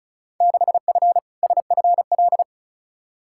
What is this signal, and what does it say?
Morse code, heard as a keyed tone, about 36 wpm, 700 Hz. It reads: 6F SFL
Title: Field Day Exchanges Encore - 35wpm